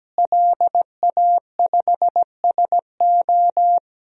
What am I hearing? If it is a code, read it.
Morse code, heard as a keyed tone, about 17 wpm, 700 Hz. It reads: LA5SO
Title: DXCC-100 Call Signs - 5 Characters - Repeated 2x - 17wpm